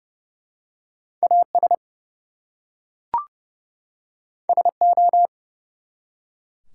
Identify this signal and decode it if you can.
Morse code, heard as a keyed tone, about 30 wpm, 700 Hz. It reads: AS SO
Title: Two Letter Words - 30wpm